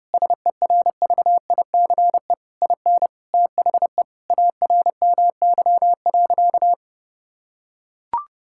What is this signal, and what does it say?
Morse code, heard as a keyed tone, about 30 wpm, 700 Hz. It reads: SERVICE IN THE ARMY.
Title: Sentences from Top 1000 Words - Review Part 6 - 30wpm